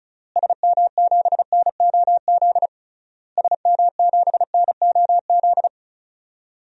Morse code, heard as a keyed tone, about 34 wpm, 700 Hz. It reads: SM7NOZ SM7NOZ
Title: DXCC-100 Call Signs - 6 Characters - Repeated 3x - 35wpm